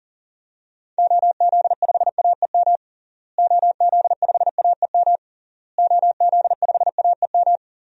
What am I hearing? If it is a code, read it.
Morse code, heard as a keyed tone, about 40 wpm, 700 Hz. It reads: OZ5AEM OZ5AEM OZ5AEM
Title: DXCC-100 Call Signs - 6 Characters - Repeated 3x - 40wpm